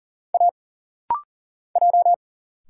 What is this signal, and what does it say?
Morse code, heard as a keyed tone, about 40 wpm, 700 Hz. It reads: A J
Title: Single Letter-Number - Rapid-Fire 40wpm